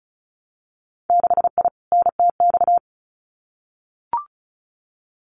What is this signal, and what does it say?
Morse code, heard as a keyed tone, about 36 wpm, 700 Hz. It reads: 6I NTX
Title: Winter Field Day Exchanges 35wpm